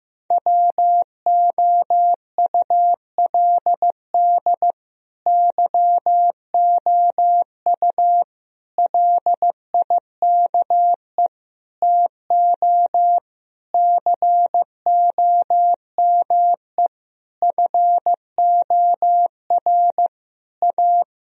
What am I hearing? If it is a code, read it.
Morse code, heard as a keyed tone, about 15 wpm, 700 Hz. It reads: WOULD YOU LIKE TO COME FOR A
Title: Sentences from Top 1000 Words - Review Part 1 - 15wpm